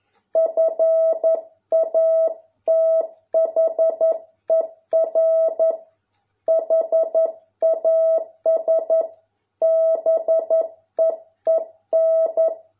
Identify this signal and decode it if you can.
Morse code, heard as a keyed tone, about 11 wpm, 650 Hz. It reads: FATHER HAS BEEN